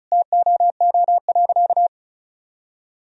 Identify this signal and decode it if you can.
Morse code, heard as a keyed tone, about 36 wpm, 700 Hz. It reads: TOO.
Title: Sentences from Top 1000 Words - Review Part 4 - 35wpm